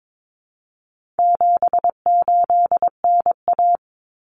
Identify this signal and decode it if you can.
Morse code, heard as a keyed tone, about 22 wpm, 700 Hz. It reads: 78NA
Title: Common Cut Numbers - 4-Digit 22wpm